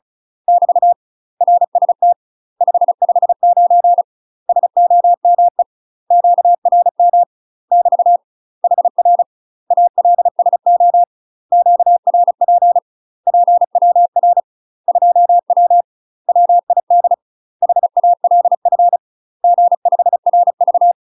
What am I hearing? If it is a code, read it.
Morse code, heard as a keyed tone, about 34 wpm, 700 Hz. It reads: <BT> RST 559 SOME QRM <BT> HR ALSO QRP PWR 2W WID HALF G5RV
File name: UK MASTER QSOs TRAINING / 35WPM / UK MASTER QSOs 0008